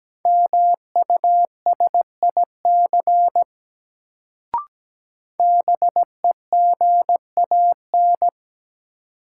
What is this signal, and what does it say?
Morse code, heard as a keyed tone, about 17 wpm, 700 Hz. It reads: MUSIC BEGAN
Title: Sentences from Top 300 Words - New Word Drill - 17wpm